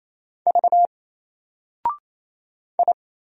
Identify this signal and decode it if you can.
Morse code, heard as a keyed tone, about 27 wpm, 700 Hz. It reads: V I
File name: Single-Letter-Number-28wpm